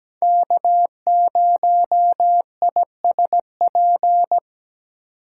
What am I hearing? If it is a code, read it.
Morse code, heard as a keyed tone, about 20 wpm, 700 Hz. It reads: K0ISP
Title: Speed Racing - USA 1x3 Call Signs 17wpm